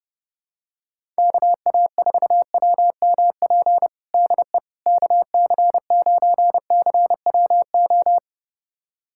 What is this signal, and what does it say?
Morse code, heard as a keyed tone, about 32 wpm, 700 Hz. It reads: KA4WMP DE KC9CWO
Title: USA Callsign DE Callsign - 6 Characters - 30wpm